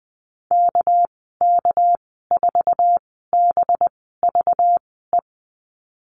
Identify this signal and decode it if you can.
Morse code, heard as a keyed tone, about 20 wpm, spaced as for 15 wpm, 700 Hz. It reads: KK4BVE
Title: 213 Lesson 34g - Number 8 - Call Signs 20wpm